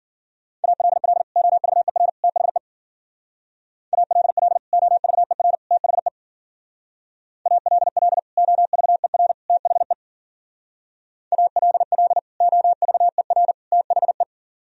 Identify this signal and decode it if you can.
Morse code, heard as a keyed tone, about 50 wpm, 700 Hz. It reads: ALL OVER THE ALL OVER THE ALL OVER THE ALL OVER THE
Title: Speed-Racing - Sets of 3 Words 40wpm